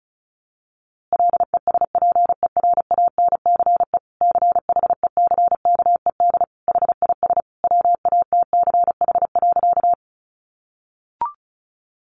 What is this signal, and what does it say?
Morse code, heard as a keyed tone, about 34 wpm, 700 Hz. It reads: LESPERANCE CHECKED HIS WATCH.
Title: A Sound Of Thunder 35wpm